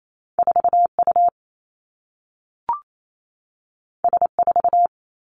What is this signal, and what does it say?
Morse code, heard as a keyed tone, about 28 wpm, 700 Hz. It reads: 4 U S 4
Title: Two Challenging Letter-Number Combinations 28wpm